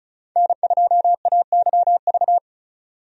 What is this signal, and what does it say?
Morse code, heard as a keyed tone, about 34 wpm, 700 Hz. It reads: N2AYV
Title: USA Call Signs - 5 Characters - 35wpm